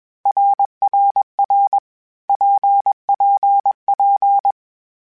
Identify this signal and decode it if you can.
Morse code, heard as a keyed tone, about 21 wpm, 800 Hz. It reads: RRR PPP
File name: Groups@20 / Groups@20 mp3s / RKPX@20